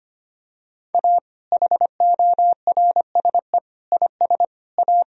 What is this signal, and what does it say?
Morse code, heard as a keyed tone, about 25 wpm, 700 Hz. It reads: A HORSE IS A
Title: Sentences from Top 1000 Words - Review Part 1 - 25wpm